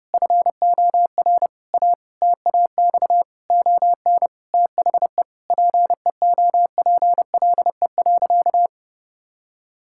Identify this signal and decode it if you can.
Morse code, heard as a keyed tone, about 30 wpm, 700 Hz. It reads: FOR A TAX ON THE PEOPLE.
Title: Sentences from Top 1000 Words - Review Part 5 - 30wpm